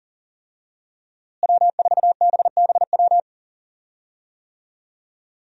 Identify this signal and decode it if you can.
Morse code, heard as a keyed tone, about 40 wpm, 700 Hz. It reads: W4BBW
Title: USA Call Signs - 5 Characters - Repeated 3x - 40wpm